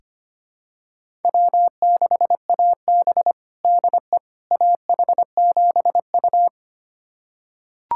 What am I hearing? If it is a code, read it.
Morse code, heard as a keyed tone, about 25 wpm, 700 Hz. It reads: W6AB DE AH7U
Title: USA Callsign DE Callsign - 4 Characters - 25wpm